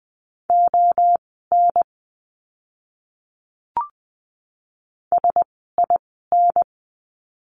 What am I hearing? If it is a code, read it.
Morse code, heard as a keyed tone, about 20 wpm, spaced as for 14 wpm, 700 Hz. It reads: ON SIN
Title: 036 Lesson 7f - Letter s - Words 20wpm